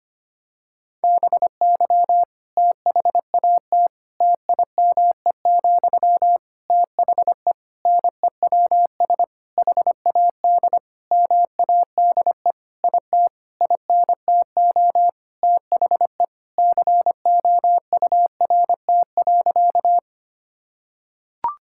Morse code, heard as a keyed tone, about 25 wpm, 700 Hz. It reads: BY THAT TIME, THE NEWS HAD MADE IT INTO THE COURT.
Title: Sentences from Top 1600 Words 25wpm